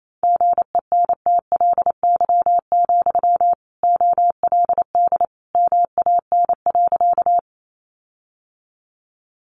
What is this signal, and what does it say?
Morse code, heard as a keyed tone, about 28 wpm, 700 Hz. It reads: GENTLY, OLD MAN.
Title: Sentences from Top 1500 Words 28wpm